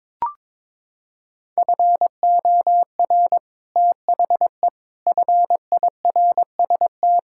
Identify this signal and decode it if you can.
Morse code, heard as a keyed tone, about 22 wpm, 700 Hz. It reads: FOR THE FIRST
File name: Sentences with 7-Word Phrases - 22wpm